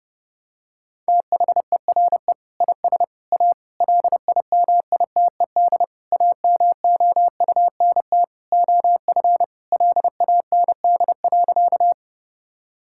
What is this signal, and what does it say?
Morse code, heard as a keyed tone, about 30 wpm, 700 Hz. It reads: THERE IS A LIMITED AMOUNT OF LAND.
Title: Sentences from Top 1000 Words - Review Part 6 - 30wpm